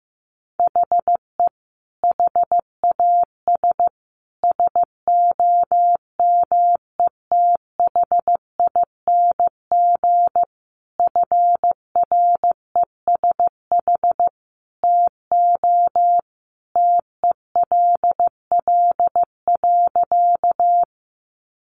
Morse code, heard as a keyed tone, about 15 wpm, 700 Hz. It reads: HE HAS SOMETHING FRESH TO TELL.
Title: Sentences from Top 1200 Words 15wpm